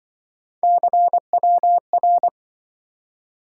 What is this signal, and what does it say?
Morse code, heard as a keyed tone, about 25 wpm, 700 Hz. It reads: CWR
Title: Speed-Racing - Three Letter-Number Combinations - 20wpm